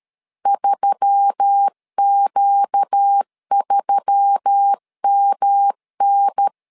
Morse code, heard as a keyed tone, about 13 wpm, 800 Hz. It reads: 3Q3MN
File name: LEZIONI / 4 Q1S3 12wpm 2023